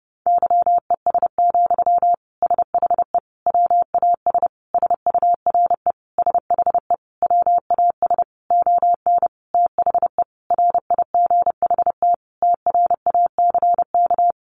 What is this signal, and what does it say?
Morse code, heard as a keyed tone, about 30 wpm, 700 Hz. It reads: YES, SHE WAS SURE SHE WAS ON THE RIGHT TRACK
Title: Sentences from Top 1100 Words 30wpm